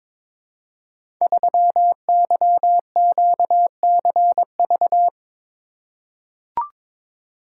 Morse code, heard as a keyed tone, about 22 wpm, 690 Hz. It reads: 3YQCV
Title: Code Groups - 5 Letter:Numbers - 22wpm